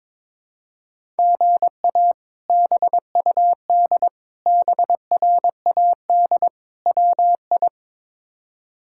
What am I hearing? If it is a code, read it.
Morse code, heard as a keyed tone, about 21 wpm, 700 Hz. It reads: GA BUD BRAD WI
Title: K1USN SST QSO Practice - 22wpm